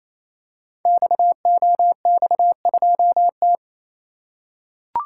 Morse code, heard as a keyed tone, about 28 wpm, 700 Hz. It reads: XOX2T
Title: Code Groups - 5 Letter:Numbers - 28wpm